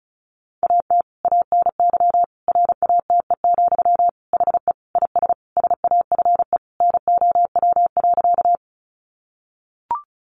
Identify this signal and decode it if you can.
Morse code, heard as a keyed tone, about 34 wpm, 700 Hz. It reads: AT ANY RATE, HE IS SAFE NOW.
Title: Sentences from Top 900 Words 35wpm